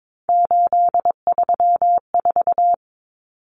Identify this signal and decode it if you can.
Morse code, heard as a keyed tone, about 22 wpm, 700 Hz. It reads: 834
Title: Numbers - 3-Digit 22wpm